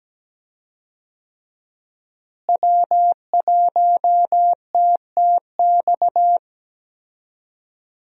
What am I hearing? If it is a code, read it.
Morse code, heard as a keyed tone, about 17 wpm, 700 Hz. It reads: W1TTX
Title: USA Call Signs - 5 Characters - Repeated 2x - 17wpm